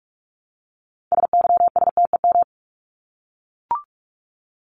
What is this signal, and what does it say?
Morse code, heard as a keyed tone, about 44 wpm, 700 Hz. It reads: SYSTEM
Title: Top 501-750 Words 45wpm